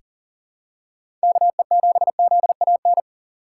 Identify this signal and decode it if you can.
Morse code, heard as a keyed tone, about 39 wpm, 700 Hz. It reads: KE7ZAN
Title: USA Call Signs - 6 Characters - 6x Character Spacing - 40wpm